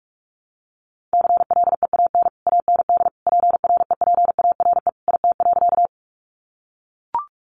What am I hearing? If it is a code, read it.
Morse code, heard as a keyed tone, about 44 wpm, 700 Hz. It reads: CLEAN AND PREPARE IT.
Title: Sentences from Top 1100 Words 45wpm